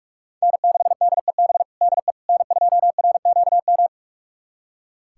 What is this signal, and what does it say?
Morse code, heard as a keyed tone, about 46 wpm, 700 Hz. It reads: N6DEB DE N2RQM
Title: USA Callsign DE Callsign - 5 Characters - 45wpm